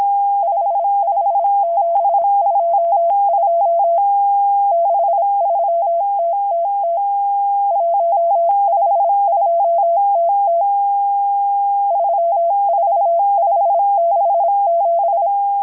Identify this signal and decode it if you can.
Morse code, heard as a keyed tone, about 26 wpm, 680 Hz. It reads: 55722 63TTT 152TT 34567
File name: HM02- -0437utc-7351khz